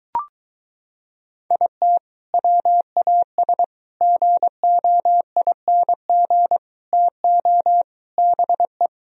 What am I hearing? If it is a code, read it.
Morse code, heard as a keyed tone, about 23 wpm, 700 Hz. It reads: IT WAS GOING TO BE